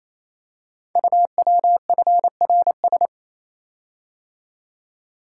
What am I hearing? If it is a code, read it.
Morse code, heard as a keyed tone, about 29 wpm, 700 Hz. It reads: UWFRS
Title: Code Groups - 5 Letters - 28wpm